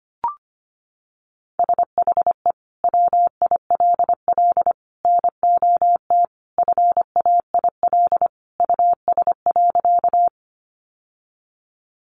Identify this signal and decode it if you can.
Morse code, heard as a keyed tone, about 25 wpm, 690 Hz. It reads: SHE WILL NOT FAIL US.
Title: Sentences from Top 1100 Words 25wpm